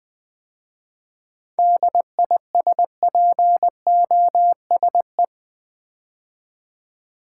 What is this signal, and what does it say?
Morse code, heard as a keyed tone, about 20 wpm, 700 Hz. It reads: DISPOSE
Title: 7 Letter Words - First 3 Letters Given - 20wpm